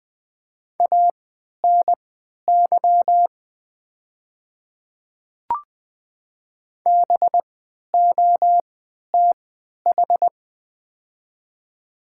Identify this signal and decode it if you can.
Morse code, heard as a keyed tone, about 20 wpm, spaced as for 11 wpm, 700 Hz. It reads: ANY BOTH
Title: 3x Character Spacing - Top 300 Words - 20wpm